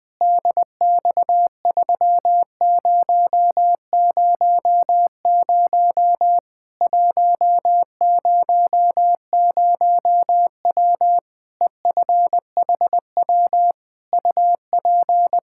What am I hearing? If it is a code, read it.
Morse code, heard as a keyed tone, about 20 wpm, 700 Hz. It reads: DX3000 100W EFHW UP
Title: K1USN SST QSO Practice 20wpm